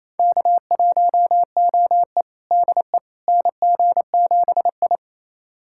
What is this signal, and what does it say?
Morse code, heard as a keyed tone, about 28 wpm, 700 Hz. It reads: K1OE DE NG7I
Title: USA Callsign DE Callsign - 4 Characters - 28wpm